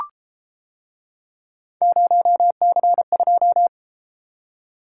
Morse code, heard as a keyed tone, about 32 wpm, 700 Hz. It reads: 0C2